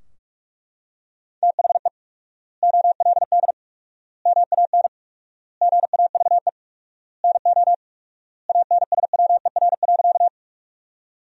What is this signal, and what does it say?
Morse code, heard as a keyed tone, about 42 wpm, 700 Hz. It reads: THE OLD MAN GAVE NO ANSWER.
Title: 4x Word-Spacing - Sentences from Top 500 Words - 45wpm